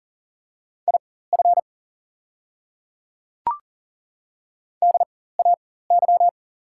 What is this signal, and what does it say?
Morse code, heard as a keyed tone, about 40 wpm, spaced as for 19 wpm, 700 Hz. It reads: IF DAY
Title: 4x Character Spacing - Top 100 Words - 40wpm